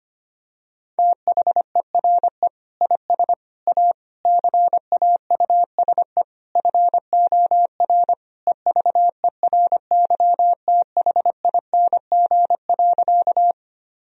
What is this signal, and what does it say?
Morse code, heard as a keyed tone, about 25 wpm, 700 Hz. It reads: THERE IS A CAUSE FOR EVERYTHING.
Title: Sentences from Top 1000 Words - Review Part 1 - 25wpm